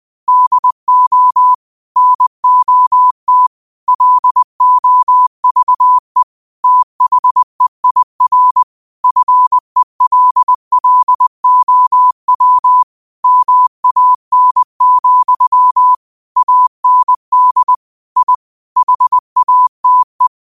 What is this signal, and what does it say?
Morse code, heard as a keyed tone, about 20 wpm, 1000 Hz. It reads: DO NOT LOVE THEIR FELLOW MAN, AND I HATE